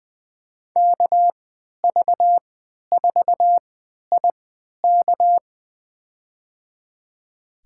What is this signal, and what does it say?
Morse code, heard as a keyed tone, about 20 wpm, spaced as for 11 wpm, 700 Hz. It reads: KV4IK
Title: USA Call Signs - 5 Characters - 3x Character Spacing - 20wpm